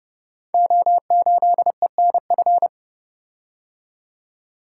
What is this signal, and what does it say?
Morse code, heard as a keyed tone, about 30 wpm, 700 Hz. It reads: O8ENF
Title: Code Groups - 5 Letter:Numbers - 30wpm